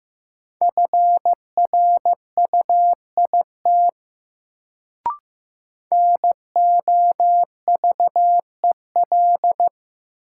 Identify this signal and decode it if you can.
Morse code, heard as a keyed tone, about 15 wpm, 700 Hz. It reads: FRUIT NOVEL
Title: Sentences from Top 1300 Words - New Word Drill Alt - 15wpm